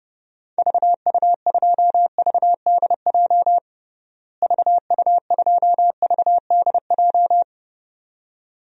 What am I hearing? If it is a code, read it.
Morse code, heard as a keyed tone, about 30 wpm, 700 Hz. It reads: VU2VDJ VU2VDJ
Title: DXCC-100 Call Signs - 6 Characters - Repeated 2x - 30wpm